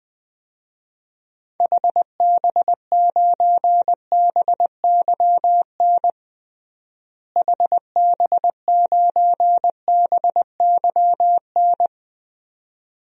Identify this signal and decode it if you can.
Morse code, heard as a keyed tone, about 21 wpm, 700 Hz. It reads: HB9BYN HB9BYN
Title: DXCC-100 Call Signs - 6 Characters - Repeated 2x - 20wpm